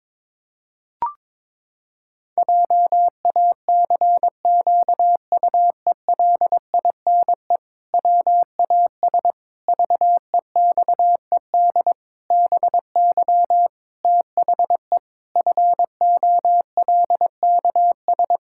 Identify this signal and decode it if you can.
Morse code, heard as a keyed tone, about 22 wpm, 700 Hz. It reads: JACQUELINE WAS VEXED BY THE FOLKS
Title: Pangram Challenge v2 - 22wpm